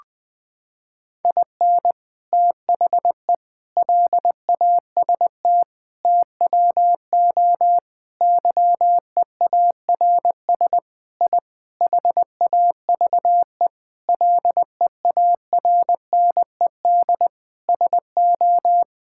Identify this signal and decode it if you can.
Morse code, heard as a keyed tone, about 20 wpm, 700 Hz. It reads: IN THE LAST TWO YEARS I HAVE LEARNED SO
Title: Sentences with 5-Word Phrases - 20wpm